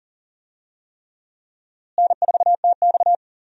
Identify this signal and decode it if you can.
Morse code, heard as a keyed tone, about 39 wpm, 700 Hz. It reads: N4TX
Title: USA Call Signs - 4 Characters - Repeated 3x - 40wpm